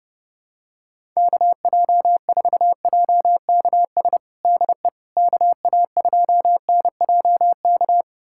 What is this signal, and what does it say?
Morse code, heard as a keyed tone, about 32 wpm, 700 Hz. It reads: KJ4JKS DE KA2NJK
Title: USA Callsign DE Callsign - 6 Characters - 30wpm